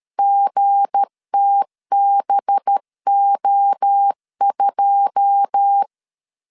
Morse code, heard as a keyed tone, about 13 wpm, 800 Hz. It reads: GTBO2
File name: LEZIONI / 9 2 12wpm 2301